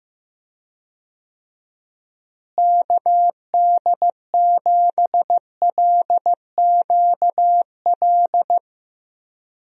Morse code, heard as a keyed tone, about 15 wpm, 700 Hz. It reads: KD7LQL
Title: USA Call Signs - 6 Characters - Repeated 2x - 15wpm